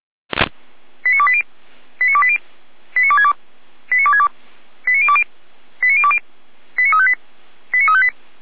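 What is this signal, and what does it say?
Five-tone selective call (ZVEI), heard as quick runs of five tones, tones of about 70 ms. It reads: 89280 89280 89262 89262 89020 89020 89358 89358
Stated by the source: ZVEI five-tone call received by SDR.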